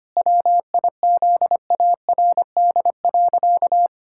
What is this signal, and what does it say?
Morse code, heard as a keyed tone, about 25 wpm, 700 Hz. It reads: WIZARD.
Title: Pangram Challenge v2 - 25wpm